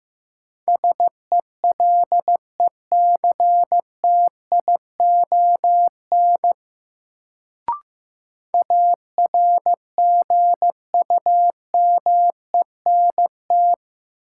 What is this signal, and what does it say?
Morse code, heard as a keyed tone, about 15 wpm, 700 Hz. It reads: SELECTION ARGUMENT
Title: Sentences from Top 1400 Words - New Word Drill Alt - 15wpm